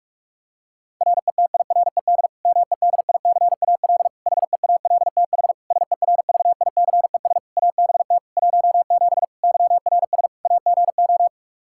Morse code, heard as a keyed tone, about 46 wpm, 700 Hz. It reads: RETIRED MEDICAL HEALTH SERVICES ABT 17 YRS AGO
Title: K1USN SST QSO Practice 45wpm